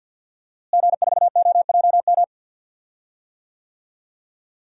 Morse code, heard as a keyed tone, about 48 wpm, 700 Hz. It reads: G4OJM
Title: DXCC-100 Call Signs - 5 Characters - Repeated 3x - 50wpm